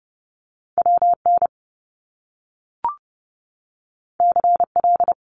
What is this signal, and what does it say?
Morse code, heard as a keyed tone, about 30 wpm, 700 Hz. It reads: WN CL
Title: Top Two Letter Combinations in Words 30wpm